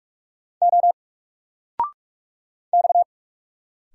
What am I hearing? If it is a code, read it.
Morse code, heard as a keyed tone, about 45 wpm, 700 Hz. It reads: O X
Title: Single Letter-Number 45wpm